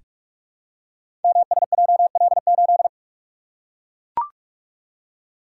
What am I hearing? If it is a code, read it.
Morse code, heard as a keyed tone, about 44 wpm, 700 Hz. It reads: MSJL8
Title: Code Groups - 5 Letter:Numbers - 45wpm